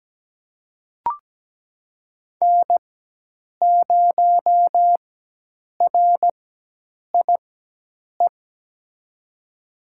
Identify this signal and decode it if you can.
Morse code, heard as a keyed tone, about 17 wpm, spaced as for 8 wpm, 700 Hz. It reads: N0RIE
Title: USA Call Signs - 5 Characters - 4x Character Spacing - 17wpm